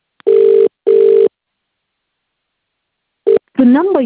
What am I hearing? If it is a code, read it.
Ringback tone (UK), 400 + 450 Hz, with speech.